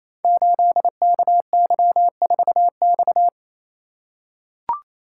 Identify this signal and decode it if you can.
Morse code, heard as a keyed tone, about 27 wpm, 700 Hz. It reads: 8KY4X
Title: Code Groups - 5 Letter:Numbers - 28wpm